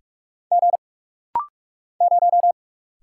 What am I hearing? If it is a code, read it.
Morse code, heard as a keyed tone, about 45 wpm, 700 Hz. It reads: G 0
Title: Single Letter-Number - Rapid-Fire 45wpm